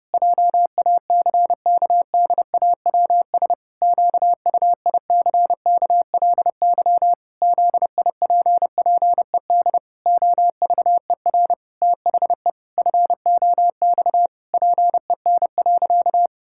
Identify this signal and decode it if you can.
Morse code, heard as a keyed tone, about 30 wpm, 700 Hz. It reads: JACKDAWS QUICKLY ZIPPED OVER THE FOX PEN.
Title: Pangram Challenge v2 - 30wpm